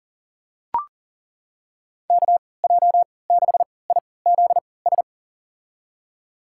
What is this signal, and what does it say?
Morse code, heard as a keyed tone, about 40 wpm, spaced as for 23 wpm, 700 Hz. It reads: KJ6IZS
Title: USA Call Signs - 6 Characters - 3x Character Spacing - 40wpm